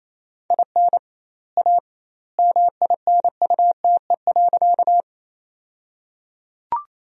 Morse code, heard as a keyed tone, about 28 wpm, 700 Hz. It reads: IN A MINUTE.
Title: 2x Word-Spacing - Sentences from Top 500 Words - 28wpm